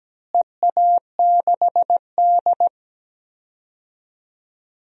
Morse code, heard as a keyed tone, about 17 wpm, 700 Hz. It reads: EA6D
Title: DXCC-100 Call Signs - 4 Characters - Repeated 3x - 17wpm